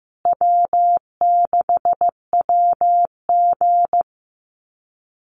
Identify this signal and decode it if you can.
Morse code, heard as a keyed tone, about 15 wpm, 700 Hz. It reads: W6WG
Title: Call Signs Repeated 2x - 15wpm